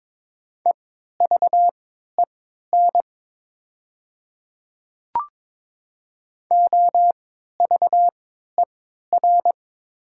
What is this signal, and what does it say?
Morse code, heard as a keyed tone, about 22 wpm, spaced as for 12 wpm, 690 Hz. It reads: EVEN OVER
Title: 3x Character Spacing - Top 500 Words - 22wpm